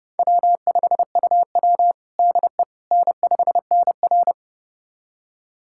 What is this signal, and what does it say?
Morse code, heard as a keyed tone, about 30 wpm, 700 Hz. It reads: W5UW DE N5NR
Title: USA Callsign DE Callsign - 4 Characters - 30wpm